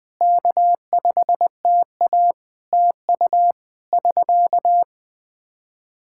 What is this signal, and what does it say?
Morse code, heard as a keyed tone, about 20 wpm, 700 Hz. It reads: K5TA TU <SK>
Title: Practice QSO - Exchange 4 - Ending - 20wpm